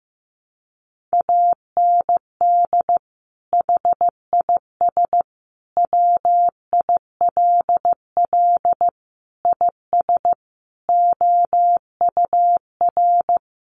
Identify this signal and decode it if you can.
Morse code, heard as a keyed tone, about 15 wpm, 700 Hz. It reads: AND HIS WILL IS OUR
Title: Sentences from Top 1500 Words 15wpm